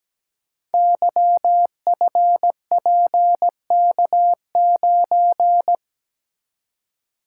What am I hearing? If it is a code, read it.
Morse code, heard as a keyed tone, about 17 wpm, 700 Hz. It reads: YFPK9
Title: Code Groups - 5 Letter:Numbers - 17wpm